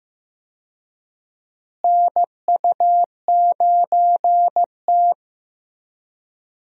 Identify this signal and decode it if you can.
Morse code, heard as a keyed tone, about 15 wpm, 700 Hz. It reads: NU9T
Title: USA Call Signs - 4 Characters - Repeated 2x - 15wpm